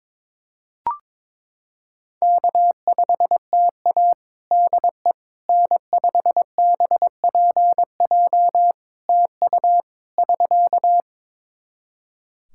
Morse code, heard as a keyed tone, about 22 wpm, 700 Hz. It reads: K5TA DE N5BPJ TU <SK>
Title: K1USN SST QSO Practice 22wpm